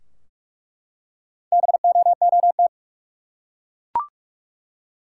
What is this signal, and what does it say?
Morse code, heard as a keyed tone, about 47 wpm, 700 Hz. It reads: BOOT
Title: Four Letter Words - 45wpm